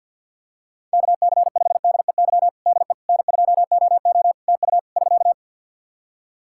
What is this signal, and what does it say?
Morse code, heard as a keyed tone, about 51 wpm, 700 Hz. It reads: KC5DEY DE N2OO TU <SK>
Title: K1USN SST QSO Practice 50wpm